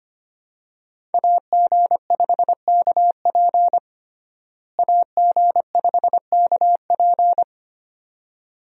Morse code, heard as a keyed tone, about 25 wpm, 700 Hz. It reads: AG5KP AG5KP
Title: USA Call Signs - 5 Characters - Repeated 2x - 25wpm